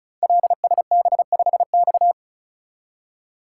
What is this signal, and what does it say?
Morse code, heard as a keyed tone, about 36 wpm, 690 Hz. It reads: LSB5X
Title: Code Groups - 5 Letter:Numbers - 35wpm